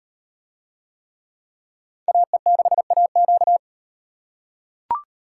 Morse code, heard as a keyed tone, about 35 wpm, 700 Hz. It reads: AE6AQ